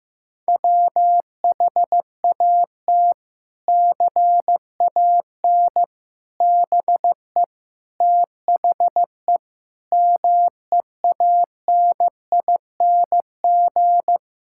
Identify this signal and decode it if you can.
Morse code, heard as a keyed tone, about 15 wpm, 700 Hz. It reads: WHAT CAN BE THE MEANING
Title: Sentences from Top 1000 Words - Review Part 6 - 15wpm